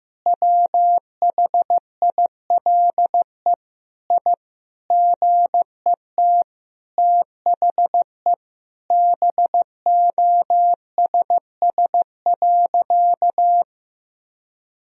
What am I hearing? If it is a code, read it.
Morse code, heard as a keyed tone, about 16 wpm, 700 Hz. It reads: WHILE I GET THE BOSS.
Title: Sentences from Top 1000 Words - Review Part 4 - 15wpm